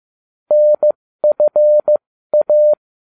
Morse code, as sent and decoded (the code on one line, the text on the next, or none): -. ..-. .-
NFA